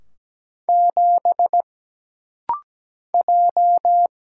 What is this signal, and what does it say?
Morse code, heard as a keyed tone, about 17 wpm, 700 Hz. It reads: 7 J
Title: Single Letter-Number Alt - Rapid-Fire 17wpm